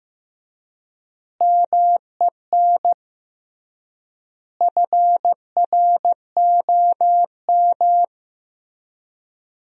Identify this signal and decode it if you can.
Morse code, heard as a keyed tone, about 15 wpm, 700 Hz. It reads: MEN FROM
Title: 3x Word-Spacing - Sentences from Top 300 Words - 15wpm